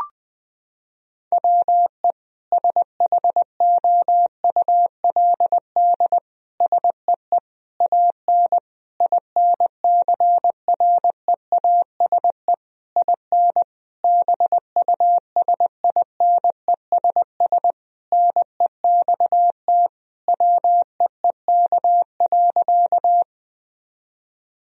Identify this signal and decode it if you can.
Morse code, heard as a keyed tone, about 20 wpm, 700 Hz. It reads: WE SHOULD SEE AN INCREASE IN BUSINESS NEXT WEEK.
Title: Sentences from Top 1000 Words - Review Part 4 - 20wpm